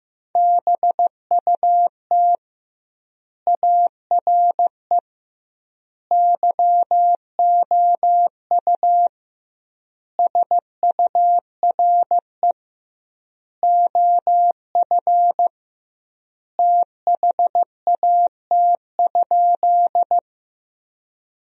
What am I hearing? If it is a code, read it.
Morse code, heard as a keyed tone, about 16 wpm, 700 Hz. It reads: BUT ARE YOU SURE OF THAT?
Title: 2x Word-Spacing - Sentences from Top 300 Words - 15wpm